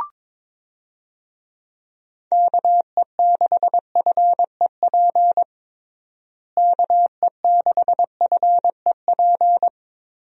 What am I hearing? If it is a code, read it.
Morse code, heard as a keyed tone, about 22 wpm, 700 Hz. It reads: KE6FEP KE6FEP
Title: USA Call Signs - 6 Characters - Repeated 2x - 22wpm